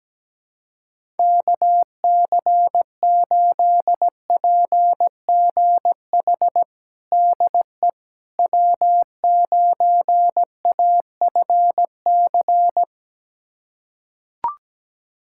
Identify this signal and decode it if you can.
Morse code, heard as a keyed tone, about 17 wpm, 700 Hz. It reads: KC8PGH DE W9AFC
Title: USA Callsign DE Callsign 17wpm